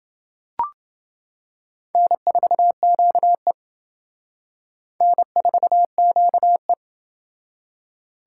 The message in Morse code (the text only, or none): N4QE N4QE